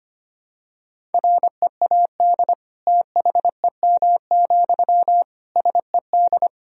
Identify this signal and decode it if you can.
Morse code, heard as a keyed tone, about 25 wpm, 700 Hz. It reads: READ THEM, SED
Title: Sentences from Top 400 Words and Abbreviations 25wpm